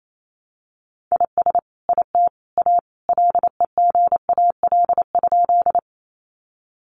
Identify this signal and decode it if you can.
Morse code, heard as a keyed tone, about 28 wpm, 700 Hz. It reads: IS IT A LEGAL?
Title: Sentences from Top 1200 Words 28wpm